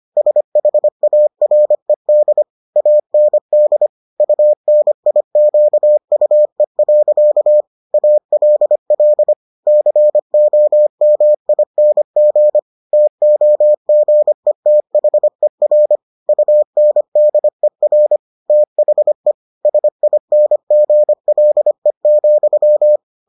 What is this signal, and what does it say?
Morse code, heard as a keyed tone, about 25 wpm, 600 Hz. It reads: SHARED AND UNIQUE. ALL COMING TOGETHER UNDER THE SINGLE,